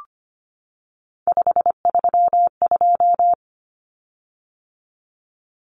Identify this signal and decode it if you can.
Morse code, heard as a keyed tone, about 25 wpm, 700 Hz. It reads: 532
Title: Common Cut Numbers - 3-Digit 25wpm